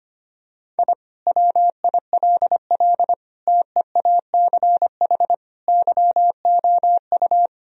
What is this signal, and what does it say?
Morse code, heard as a keyed tone, about 25 wpm, 700 Hz. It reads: I WILL TEACH YOU
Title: Sentences from Top 1000 Words - Review Part 3 - 25wpm